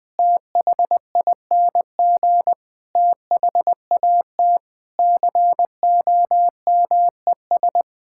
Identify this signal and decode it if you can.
Morse code, heard as a keyed tone, about 20 wpm, 700 Hz. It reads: THING THAT COMES
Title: Sentences with 8-Word Phrases - 20wpm